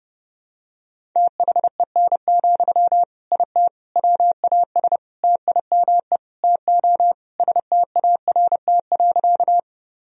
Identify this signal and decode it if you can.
Morse code, heard as a keyed tone, about 32 wpm, 700 Hz. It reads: THEN, IT WAS TIME TO START.
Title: Sentences from Top 1000 Words - Review Part 1 - 30wpm